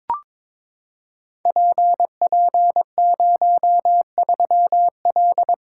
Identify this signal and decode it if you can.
Morse code, heard as a keyed tone, about 22 wpm, 700 Hz. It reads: PP03L
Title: Code Groups - 5 Letter:Numbers - 22wpm